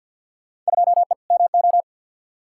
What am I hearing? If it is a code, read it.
Morse code, heard as a keyed tone, about 52 wpm, 700 Hz. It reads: 2E MO
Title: Field Day Exchanges - 50wpm